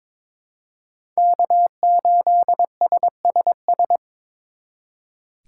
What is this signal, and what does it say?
Morse code, heard as a keyed tone, about 26 wpm, 700 Hz. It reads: K8SSS
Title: Speed Racing - USA 1x3 Call Signs 22wpm